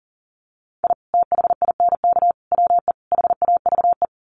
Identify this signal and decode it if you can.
Morse code, heard as a keyed tone, about 40 wpm, 700 Hz. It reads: I THINK WE HAVE
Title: Sets of 4 Words 40wpm